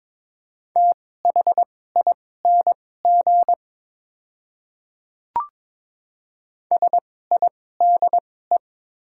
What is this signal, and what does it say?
Morse code, heard as a keyed tone, about 22 wpm, spaced as for 16 wpm, 700 Hz. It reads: THING SIDE
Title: 2x Character Spacing - Top 100 Words - 22wpm